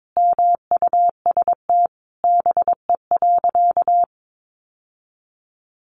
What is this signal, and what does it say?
Morse code, heard as a keyed tone, about 22 wpm, 700 Hz. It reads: MUST BE.
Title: Sentences from Top 700 Words 22wpm